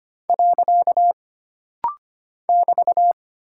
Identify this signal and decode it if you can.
Morse code, heard as a keyed tone, about 25 wpm, 700 Hz. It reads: . <BT>
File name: Single Punctuation-Prosign - Rapid-Fire - 25wpm